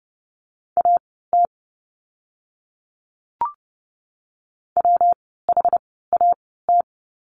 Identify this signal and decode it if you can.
Morse code, heard as a keyed tone, about 30 wpm, spaced as for 17 wpm, 700 Hz. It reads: AT WHAT
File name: course / 30:10wpm / 230 Lesson 37f - Letter x - Words 30wpm